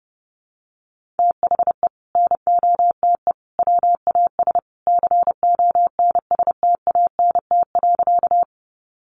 Morse code, heard as a keyed tone, about 30 wpm, 700 Hz. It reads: THE NOTE WAS CONSTANT.
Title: Sentences from Top 1500 Words 30wpm